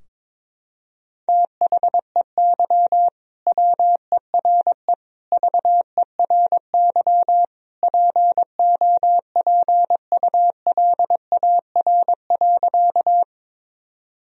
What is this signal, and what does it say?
Morse code, heard as a keyed tone, about 22 wpm, 700 Hz. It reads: THEY WERE VERY POPULAR.
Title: Sentences from Top 1000 Words - Review Part 3 - 22wpm